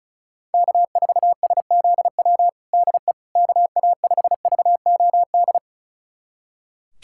Morse code, heard as a keyed tone, about 34 wpm, 700 Hz. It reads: K4SZW DE KA5VOD
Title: USA Callsign DE Callsign 35wpm